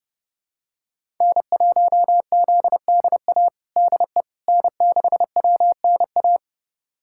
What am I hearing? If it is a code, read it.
Morse code, heard as a keyed tone, about 32 wpm, 700 Hz. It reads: N1ZDA DE N6WNA
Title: USA Callsign DE Callsign - 5 Characters - 30wpm